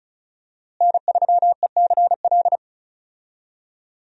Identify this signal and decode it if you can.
Morse code, heard as a keyed tone, about 36 wpm, 700 Hz. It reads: N3ECL
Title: USA Call Signs - 5 Characters - Repeated 3x - 35wpm